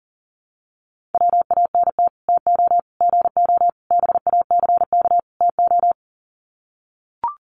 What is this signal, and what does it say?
Morse code, heard as a keyed tone, about 40 wpm, 700 Hz. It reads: WANT TO GO BACK TO
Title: Sets of 5 Words 40wpm